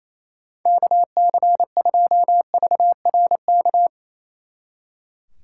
Morse code, heard as a keyed tone, about 36 wpm, 700 Hz. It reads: KC2VRK